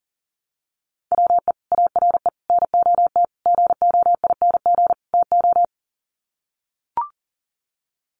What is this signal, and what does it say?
Morse code, heard as a keyed tone, about 40 wpm, 700 Hz. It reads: WE ARE NOT GOING TO
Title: Sets of 5 Words 40wpm